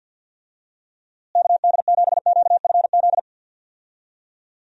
Morse code, heard as a keyed tone, about 48 wpm, 700 Hz. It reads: KD7QFZ
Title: USA Call Signs - 6 Characters - Repeated 3x - 50wpm